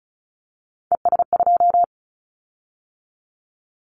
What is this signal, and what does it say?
Morse code, heard as a keyed tone, about 36 wpm, 700 Hz. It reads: ES2
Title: Callsign Prefixes for Top 50 DXCC - 35wpm